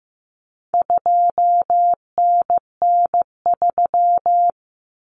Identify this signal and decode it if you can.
Morse code, heard as a keyed tone, about 15 wpm, 700 Hz. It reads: 2NN3
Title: Common Cut Numbers - 4-Digit 15wpm